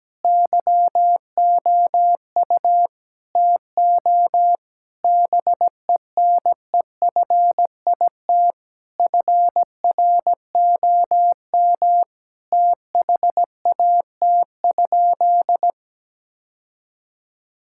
Morse code, heard as a keyed tone, about 17 wpm, 700 Hz. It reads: YOU TO BENEFIT FROM THAT?
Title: Sentences from Top 1000 Words - Review Part 6 - 17wpm